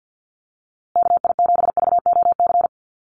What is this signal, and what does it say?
Morse code, heard as a keyed tone, about 50 wpm, 700 Hz. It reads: KI7VOC
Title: USA Call Signs Repeated 2x 50wpm